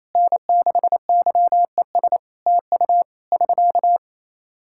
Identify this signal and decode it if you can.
Morse code, heard as a keyed tone, about 28 wpm, 700 Hz. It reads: N6YES TU <SK>
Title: K1USN SST QSO Practice 28wpm